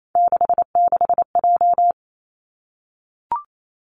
Morse code, heard as a keyed tone, about 28 wpm, 700 Hz. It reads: 66J
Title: Three Challenging Letter-Number Combinations 28wpm